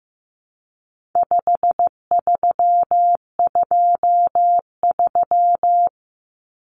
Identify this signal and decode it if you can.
Morse code, heard as a keyed tone, about 15 wpm, 700 Hz. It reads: 5323
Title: Numbers - 4-Digit 15wpm